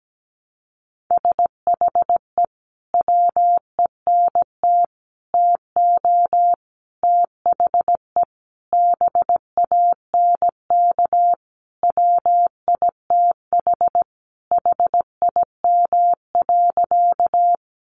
Morse code, heard as a keyed tone, about 17 wpm, 700 Hz. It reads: SHE WENT TO THE BANK WITH HIM.
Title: Sentences from Top 900 Words 17wpm